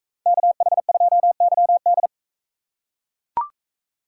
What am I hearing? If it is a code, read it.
Morse code, heard as a keyed tone, about 41 wpm, 700 Hz. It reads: KH2YD